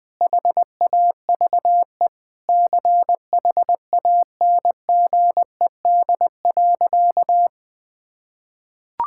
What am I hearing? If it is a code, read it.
Morse code, heard as a keyed tone, about 21 wpm, 700 Hz. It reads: HAVE CHANGED.
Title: Sentences with 5-Word Phrases - 20wpm